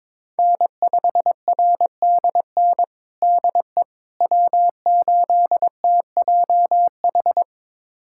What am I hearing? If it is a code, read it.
Morse code, heard as a keyed tone, about 22 wpm, 700 Hz. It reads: N5RDN DE W8TJH
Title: USA Callsign DE Callsign - 5 Characters - 22wpm